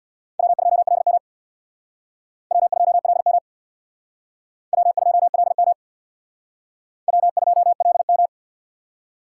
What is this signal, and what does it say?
Morse code, heard as a keyed tone, about 60 wpm, 700 Hz. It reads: W2LM W2LM W2LM W2LM
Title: Speed Racing - USA 1x2 Call Signs 50wpm